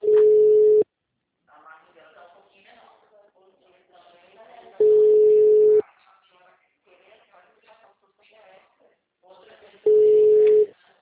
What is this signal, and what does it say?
Ringback tone (Europe, 425 Hz), with speech.